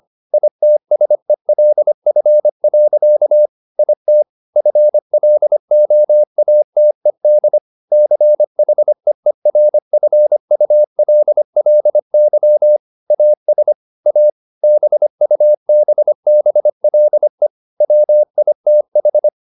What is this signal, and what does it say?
Morse code, heard as a keyed tone, about 25 wpm, 600 Hz. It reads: ITSELF. IT FLOATED CHEERFULLY AS A BUBBLE WITH
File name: the old man and the sea part 1 25wpm